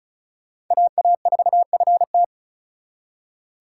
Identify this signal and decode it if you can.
Morse code, heard as a keyed tone, about 45 wpm, 700 Hz. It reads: AA4FT